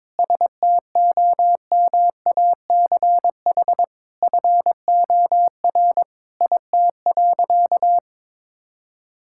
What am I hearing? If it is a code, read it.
Morse code, heard as a keyed tone, about 22 wpm, 700 Hz. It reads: STOMACH FOR IT.
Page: morse-code-ninja-practice-22wpm